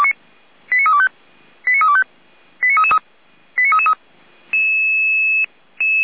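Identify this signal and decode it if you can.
Five-tone selective call (ZVEI), heard as quick runs of five tones, tones of about 80 ms. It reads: 89325 89325 89333 89333
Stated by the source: Five-tone ZVEI selective call 89322, received by SDR.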